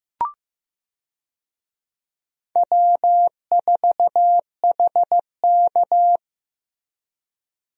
Morse code, heard as a keyed tone, about 15 wpm, 700 Hz. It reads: W4HK
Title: USA Call Signs - 4 Characters - 15wpm